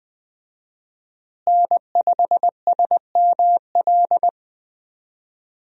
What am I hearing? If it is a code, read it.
Morse code, heard as a keyed tone, about 21 wpm, 700 Hz. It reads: N5SML
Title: USA Call Signs - 5 Characters - Repeated 2x - 20wpm